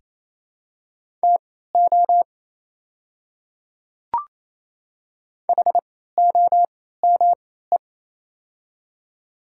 Morse code, heard as a keyed tone, about 28 wpm, spaced as for 16 wpm, 700 Hz. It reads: TO HOME
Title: 3x Character Spacing - Top 500 Words - 28wpm